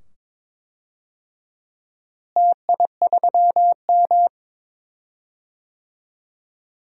Morse code, heard as a keyed tone, about 22 wpm, 700 Hz. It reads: TI3M
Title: DXCC-100 Call Signs - 4 Characters - Repeated 2x - 22wpm